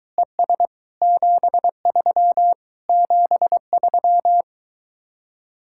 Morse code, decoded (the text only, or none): ES 73 73